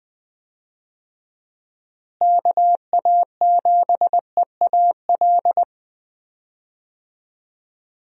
Morse code, heard as a keyed tone, about 20 wpm, 700 Hz. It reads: KA7EAL
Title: USA Call Signs - 6 Characters - Repeated 3x - 20wpm